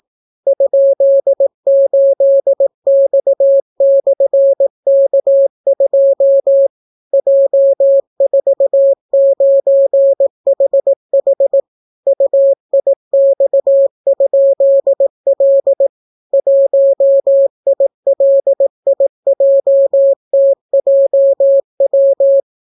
Morse code, heard as a keyed tone, about 18 wpm, 550 Hz. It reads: ?8X/K2 J49HH UIX?L 1ILIJTJW